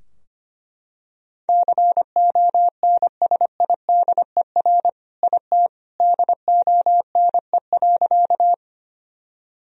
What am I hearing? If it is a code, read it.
Morse code, heard as a keyed tone, about 25 wpm, 700 Hz. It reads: CONSIDER IT DONE.
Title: Sentences from Top 1000 Words - Review Part 4 - 25wpm